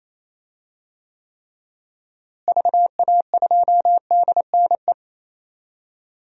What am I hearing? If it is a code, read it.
Morse code, heard as a keyed tone, about 28 wpm, 700 Hz. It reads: VA2DNE
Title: DXCC-100 Call Signs - 6 Characters - Repeated 2x - 28wpm